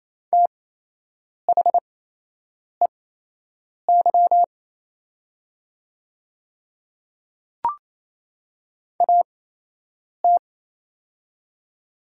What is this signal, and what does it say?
Morse code, heard as a keyed tone, about 28 wpm, spaced as for 8 wpm, 700 Hz. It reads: THEY AT
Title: 8x Character Spacing - Top 100 Words - 28wpm